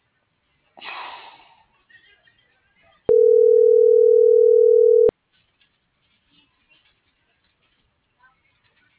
Ringback tone (North America), 440 + 480 Hz.